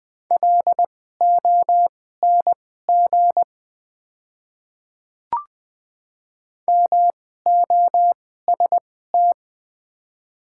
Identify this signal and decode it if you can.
Morse code, heard as a keyed tone, about 20 wpm, spaced as for 14 wpm, 700 Hz. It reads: LONG MOST
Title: 2x Character Spacing - Top 100 Words - 20wpm